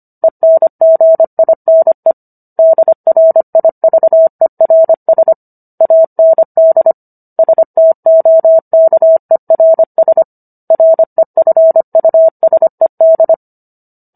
Morse code, heard as a keyed tone, about 25 wpm, 650 Hz. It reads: ENGINE DRIVERS AND STOKERS REFUSED